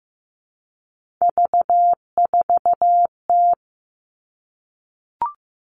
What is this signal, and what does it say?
Morse code, heard as a keyed tone, about 15 wpm, 700 Hz. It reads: V4T
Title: Three Challenging Letter-Number Combinations 15wpm